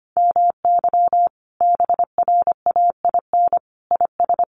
Morse code, heard as a keyed tone, about 25 wpm, 700 Hz. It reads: MY BRAIN IS
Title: Sentences from Top 1100 Words 25wpm